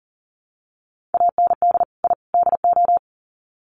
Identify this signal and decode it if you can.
Morse code, heard as a keyed tone, about 40 wpm, 700 Hz. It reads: AND I DO
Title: Sets of 3 Words 40wpm